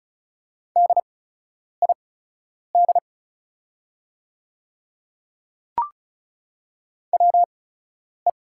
Morse code, heard as a keyed tone, about 34 wpm, spaced as for 10 wpm, 690 Hz. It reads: DID WE
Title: 8x Character Spacing - Top 300 Words - 35wpm